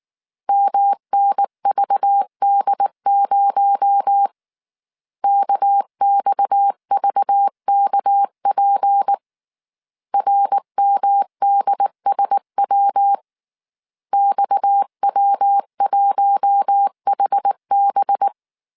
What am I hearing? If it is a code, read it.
Morse code, heard as a keyed tone, about 19 wpm, 800 Hz. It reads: MNVD0 KXVKP RMDSW XW1HB
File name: CW / DET / 1 6 1500C S C 18wpm 2356